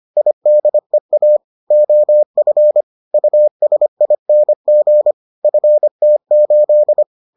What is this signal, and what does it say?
Morse code, heard as a keyed tone, about 25 wpm, 600 Hz. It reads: IDEA OF USING FT8